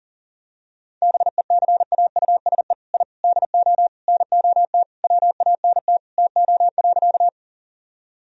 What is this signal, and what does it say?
Morse code, heard as a keyed tone, about 40 wpm, 700 Hz. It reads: BECAUSE I DO NOT WANT TO.
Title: Sentences from Top 1000 Words - Review Part 3 - 40wpm